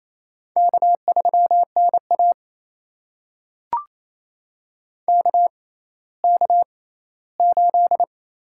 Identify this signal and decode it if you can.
Morse code, heard as a keyed tone, about 28 wpm, spaced as for 16 wpm, 700 Hz. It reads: K3NA KK8
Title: USA Call Signs - 4 Characters - 6x Character Spacing - 28wpm